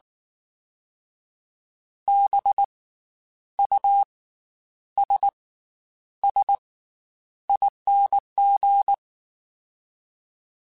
Morse code, heard as a keyed tone, about 19 wpm, 800 Hz. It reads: B U S S ING
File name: Special INGs / Special INGs-mp3s / Slow ing @18